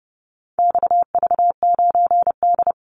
Morse code, heard as a keyed tone, about 30 wpm, 700 Hz. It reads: XV9D
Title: Call Signs - 30wpm